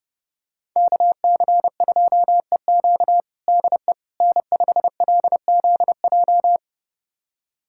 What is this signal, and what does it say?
Morse code, heard as a keyed tone, about 30 wpm, 700 Hz. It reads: KC2EQ DE N5LZJ
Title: USA Callsign DE Callsign - 5 Characters - 30wpm